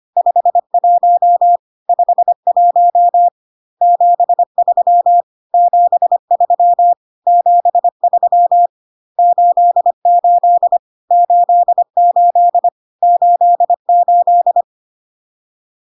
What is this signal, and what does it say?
Morse code, heard as a keyed tone, about 25 wpm, 700 Hz. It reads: 51 51 73 73 73 88 88 88
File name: cw / CW common abbreviations alphabetical repeated 3 times 25 25wpm ik1hge